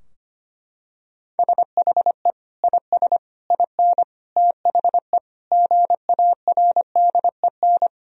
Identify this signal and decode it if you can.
Morse code, heard as a keyed tone, about 25 wpm, 700 Hz. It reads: SHE IS IN THE GARDEN
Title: Sentences from Top 1000 Words - Review Part 4 - 25wpm